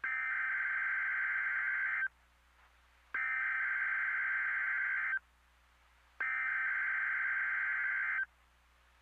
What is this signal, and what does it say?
Emergency Alert System (SAME) tones (header bursts), heard as harsh buzzing bursts. It is heard from 0.0 to 8.3 s.
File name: NOAA Weather Radio MKE-KEC60 Weekly Test